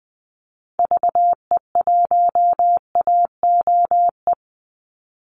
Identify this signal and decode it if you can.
Morse code, heard as a keyed tone, about 20 wpm, 700 Hz. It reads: VE1AOE
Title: Call Signs Repeated 3x - 20wpm